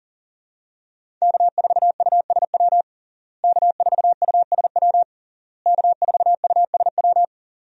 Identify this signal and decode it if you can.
Morse code, heard as a keyed tone, about 40 wpm, 700 Hz. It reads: K4USW K4USW K4USW
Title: USA Call Signs - 5 Characters - Repeated 3x - 40wpm